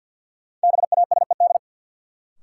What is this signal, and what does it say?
Morse code, heard as a keyed tone, about 52 wpm, 700 Hz. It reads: BASED
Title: Sets of 2 Words - New Word Drill - 50wpm